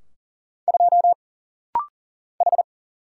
Morse code, heard as a keyed tone, about 40 wpm, 700 Hz. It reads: 2 H
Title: Single Letter-Number Alt - Rapid-Fire 40wpm